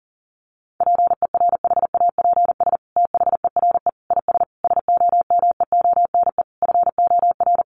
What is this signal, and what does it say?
Morse code, heard as a keyed tone, about 40 wpm, 700 Hz. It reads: PERHAPS THERE IS SOMEONE FOR
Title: Sentences from Top 600 Words 40wpm